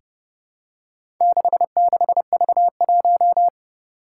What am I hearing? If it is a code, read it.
Morse code, heard as a keyed tone, about 30 wpm, 700 Hz. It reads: 66V1
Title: Four Challenging Letter-Number Combinations 30wpm